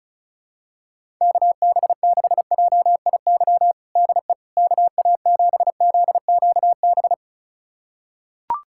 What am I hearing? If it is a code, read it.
Morse code, heard as a keyed tone, about 34 wpm, 700 Hz. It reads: KB6JIY DE KA7ZQB
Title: USA Callsign DE Callsign - 6 Characters - 35wpm